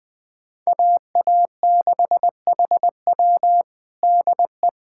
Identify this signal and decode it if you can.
Morse code, heard as a keyed tone, about 20 wpm, 700 Hz. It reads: AA6HW DE
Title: USA Callsign DE Callsign 20wpm